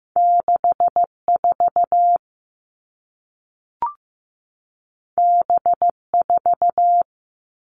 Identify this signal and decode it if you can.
Morse code, heard as a keyed tone, about 15 wpm, 700 Hz. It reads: 64 B4
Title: Two Challenging Letter-Number Combinations 15wpm